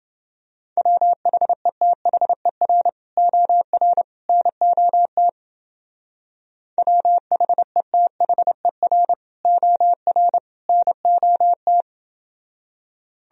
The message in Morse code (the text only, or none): WHETHER OR NOT WHETHER OR NOT